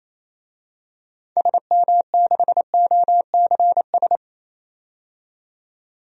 Morse code, heard as a keyed tone, about 28 wpm, 700 Hz. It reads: SM6OCS
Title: DXCC-100 Call Signs - 6 Characters - 28wpm